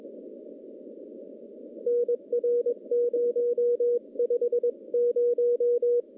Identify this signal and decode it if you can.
Morse code, heard as a keyed tone, about 20 wpm, 480 Hz. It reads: NR 050
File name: contests / CW-SS-k5tr- -0000